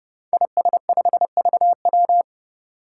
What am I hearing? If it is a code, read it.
Morse code, heard as a keyed tone, about 30 wpm, 700 Hz. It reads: IS5VW
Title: Five Challenging Letter-Number Combinations 30wpm